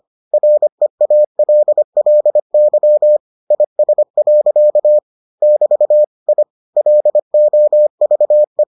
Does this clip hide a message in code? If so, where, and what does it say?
Yes: Morse code from 0.3 to 8.6 s, heard as a keyed tone, about 25 wpm, 600 Hz. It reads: REALLY IS. <BT> I LOVE